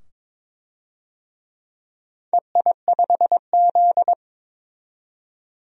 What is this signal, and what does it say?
Morse code, heard as a keyed tone, about 23 wpm, 700 Hz. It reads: EI5Z
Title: DXCC-100 Call Signs - 4 Characters - 22wpm